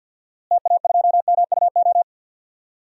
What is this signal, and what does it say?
Morse code, heard as a keyed tone, about 49 wpm, 700 Hz. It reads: TA2MUO
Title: DXCC-100 Call Signs - 6 Characters - Repeated 3x - 50wpm